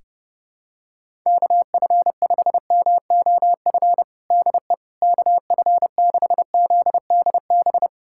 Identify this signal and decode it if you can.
Morse code, heard as a keyed tone, about 30 wpm, 700 Hz. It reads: KF5MOF DE KF6ZDB
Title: USA Callsign DE Callsign - 6 Characters - 30wpm